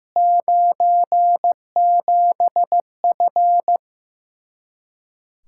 Morse code, heard as a keyed tone, about 15 wpm, 700 Hz. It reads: 97F
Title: Speed-Racing Common QSO Phrases Encore - 15wpm